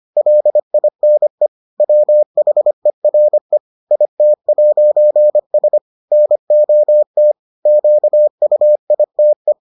Morse code, heard as a keyed tone, about 25 wpm, 600 Hz. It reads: LINE WHERE IT'S NOT QUITE